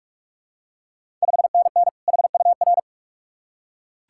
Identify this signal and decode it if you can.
Morse code, heard as a keyed tone, about 44 wpm, 690 Hz. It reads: 5NN HUR
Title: Ontario QSO Party Exchange 45wpm